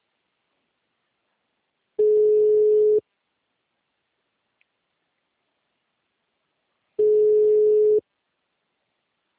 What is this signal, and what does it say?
Ringback tone (Europe, 425 Hz).